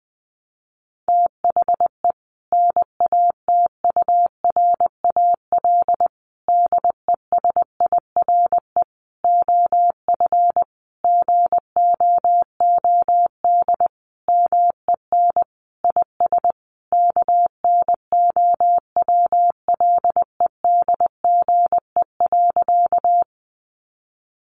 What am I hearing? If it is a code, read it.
Morse code, heard as a keyed tone, about 20 wpm, 700 Hz. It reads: THE NATURAL DESIRE OF GOOD MEN IS KNOWLEDGE.
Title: Sentences from Top 1100 Words 20wpm